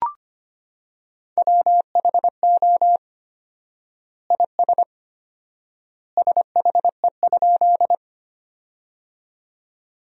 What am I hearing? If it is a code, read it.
Morse code, heard as a keyed tone, about 25 wpm, 700 Hz. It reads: WHO IS SHE?
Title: 4x Word-Spacing - Sentences from Top 100 Words - 25wpm